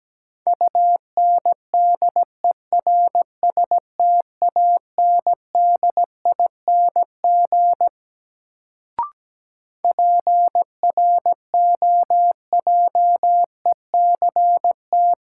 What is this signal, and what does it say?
Morse code, heard as a keyed tone, about 17 wpm, 700 Hz. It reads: UNDERSTANDING PROJECT
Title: Sentences from Top 1400 Words - New Word Drill Alt - 17wpm